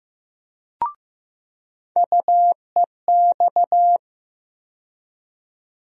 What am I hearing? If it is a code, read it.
Morse code, heard as a keyed tone, about 15 wpm, 700 Hz. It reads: UEX
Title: USA Call Sign Postfixes - 3 Letters 15wpm